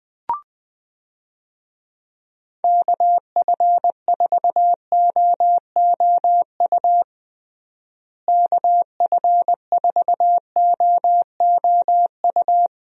Morse code, heard as a keyed tone, about 21 wpm, 700 Hz. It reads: KF4OOU KF4OOU
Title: USA Call Signs - 6 Characters - Repeated 2x - 20wpm